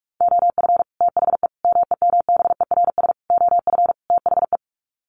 Morse code, heard as a keyed tone, about 44 wpm, 700 Hz. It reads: OF THE MEMBERS OF THE
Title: Sets of 5 Words Encore2 45wpm